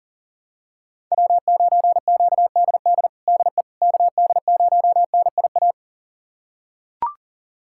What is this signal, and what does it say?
Morse code, heard as a keyed tone, about 37 wpm, 700 Hz. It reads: W9QDD DE KD0NIA
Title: USA Callsign DE Callsign 40wpm